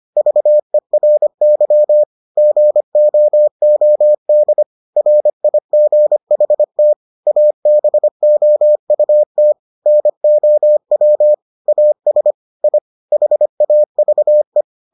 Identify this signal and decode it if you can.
Morse code, heard as a keyed tone, about 25 wpm, 600 Hz. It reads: VERY GOOD RIGHT ABOUT NOW AS I HAVE